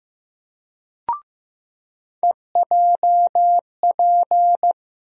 Morse code, heard as a keyed tone, about 15 wpm, 700 Hz. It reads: EJP
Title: USA Call Sign Postfixes - 3 Letters 15wpm